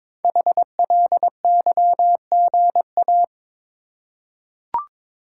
Morse code, heard as a keyed tone, about 22 wpm, 700 Hz. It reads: HLYGA
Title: Code Groups - 5 Letters - 22wpm